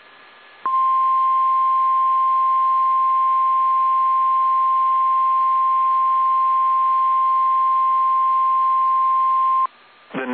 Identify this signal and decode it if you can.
Emergency Alert System (SAME) tones (the 1050 Hz attention tone). It is heard from 0.9 to 9.4 s.